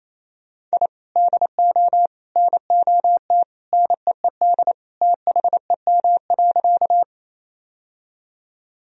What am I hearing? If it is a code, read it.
Morse code, heard as a keyed tone, about 27 wpm, 700 Hz. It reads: I DO NOT NEED THEM.
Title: Sentences from Top 1000 Words - Review Part 1 - 28wpm